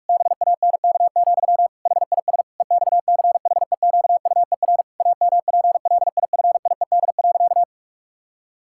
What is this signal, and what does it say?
Morse code, heard as a keyed tone, about 41 wpm, 700 Hz. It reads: BANK, HIS EXCHEQUER AMPLIFIED.
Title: Pangram Challenge v2 - 45wpm